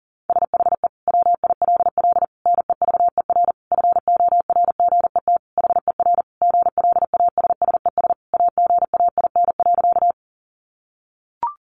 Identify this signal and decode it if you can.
Morse code, heard as a keyed tone, about 40 wpm, 700 Hz. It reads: SHE WILL NEVER FORGET HER GLASSES AGAIN.
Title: Additional Easy Phrases 40wpm